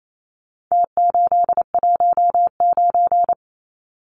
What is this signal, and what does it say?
Morse code, heard as a keyed tone, about 28 wpm, 700 Hz. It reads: T819
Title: Numbers - Contest Style - 4-Digit 28wpm